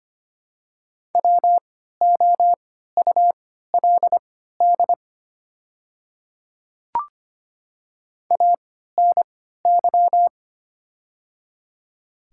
Morse code, heard as a keyed tone, about 25 wpm, spaced as for 14 wpm, 700 Hz. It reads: WOULD ANY
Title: 3x Character Spacing - Top 200 Words - 25wpm